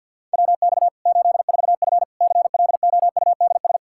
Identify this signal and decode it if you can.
Morse code, heard as a keyed tone, about 51 wpm, 700 Hz. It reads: WX 84F CLOUDS
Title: Practice QSO - Exchange 2 - Info - 50wpm